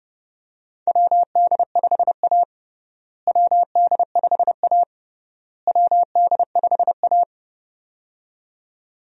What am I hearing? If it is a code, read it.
Morse code, heard as a keyed tone, about 30 wpm, 700 Hz. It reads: WD5A WD5A WD5A
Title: USA Call Signs - 4 Characters - Repeated 3x - 30wpm